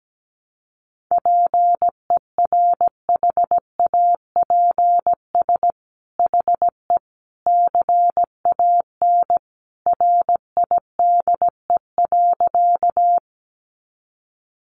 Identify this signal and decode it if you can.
Morse code, heard as a keyed tone, about 17 wpm, 700 Hz. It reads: PERHAPS HE CAN RIDE.
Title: Sentences from Top 1200 Words 17wpm